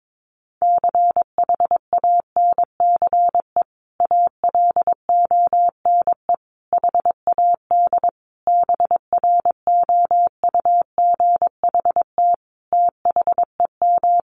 Morse code, heard as a keyed tone, about 22 wpm, 700 Hz. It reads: CHANCE ALONE HAD BROUGHT THEM
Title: Sentences from Top 1000 Words 22wpm